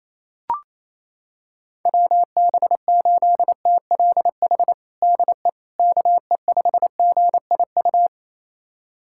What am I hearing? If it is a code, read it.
Morse code, heard as a keyed tone, about 28 wpm, 700 Hz. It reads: WB8TLH DE KE5GIU
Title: USA Callsign DE Callsign - 6 Characters - 28wpm